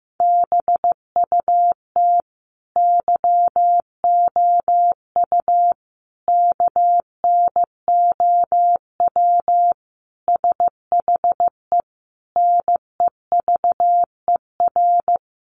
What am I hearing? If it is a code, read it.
Morse code, heard as a keyed tone, about 15 wpm, 700 Hz. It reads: BUT YOU KNOW SHE NEVER
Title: Sentences from Top 1400 Words 15wpm